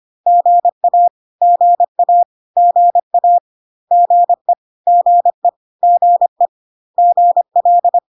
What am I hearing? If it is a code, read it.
Morse code, heard as a keyed tone, about 25 wpm, 700 Hz. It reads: GA GA GA GE GE GE GL
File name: cw / CW common abbreviations alphabetical repeated 3 times 25 25wpm ik1hge